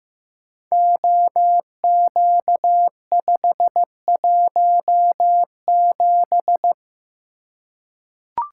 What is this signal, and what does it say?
Morse code, heard as a keyed tone, about 15 wpm, 700 Hz. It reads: OQ517
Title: Code Groups - 5 Letter:Numbers - 15wpm